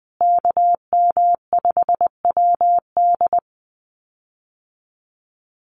Morse code, heard as a keyed tone, about 20 wpm, 700 Hz. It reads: KM5WD
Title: USA Call Signs Repeated 2x 20wpm